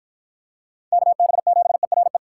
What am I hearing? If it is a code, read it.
Morse code, heard as a keyed tone, about 50 wpm, 700 Hz. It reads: KB7ERE